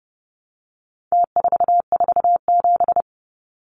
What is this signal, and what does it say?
Morse code, heard as a keyed tone, about 30 wpm, 700 Hz. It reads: T447
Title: Numbers - Contest Style - 4-Digit 30wpm